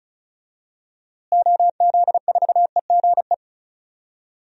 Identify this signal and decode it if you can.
Morse code, heard as a keyed tone, about 34 wpm, 700 Hz. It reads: OZ4EGE
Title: DXCC-100 Call Signs - 6 Characters - Repeated 3x - 35wpm